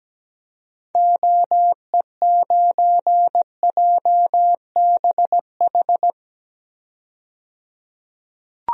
Morse code, heard as a keyed tone, about 17 wpm, 700 Hz. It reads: OE9JBH
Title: DXCC-100 Call Signs - 6 Characters - 17wpm